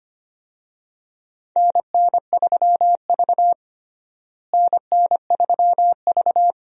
Morse code, heard as a keyed tone, about 25 wpm, 700 Hz. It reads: NN3V NN3V
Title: USA Call Signs - 4 Characters - Repeated 2x - 25wpm